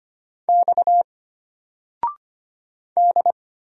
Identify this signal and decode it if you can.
Morse code, heard as a keyed tone, about 25 wpm, 700 Hz. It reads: X D
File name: Single-Letter-Number-25wpm